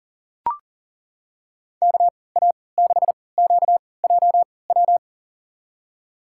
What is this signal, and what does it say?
Morse code, heard as a keyed tone, about 40 wpm, spaced as for 23 wpm, 700 Hz. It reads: KA6QJW
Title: USA Call Signs - 3x Character Spacing - 40wpm